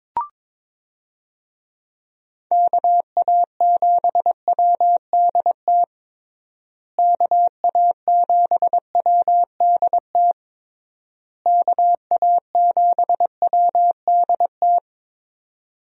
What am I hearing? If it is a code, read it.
Morse code, heard as a keyed tone, about 22 wpm, 700 Hz. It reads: KA7WDT KA7WDT KA7WDT
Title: USA Call Signs - 6 Characters - Repeated 3x - 22wpm